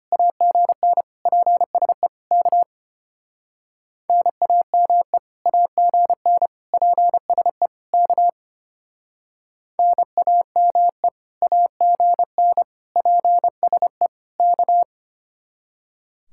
Morse code, heard as a keyed tone, about 33 wpm, 700 Hz. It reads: AGN PSE K NAME AGN PSE K NAME AGN PSE K